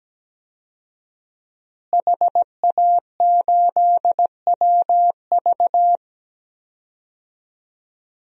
Morse code, heard as a keyed tone, about 17 wpm, 700 Hz. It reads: HA8WV
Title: DXCC-100 Call Signs - 5 Characters - Repeated 2x - 17wpm